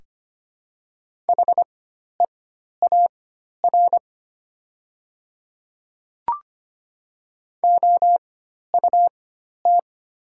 Morse code, heard as a keyed tone, about 25 wpm, spaced as for 12 wpm, 700 Hz. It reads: HEAR OUT
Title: 4x Character Spacing - Top 300 Words - 25wpm